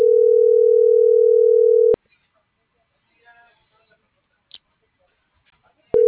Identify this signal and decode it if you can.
Ringback tone (North America), 440 + 480 Hz.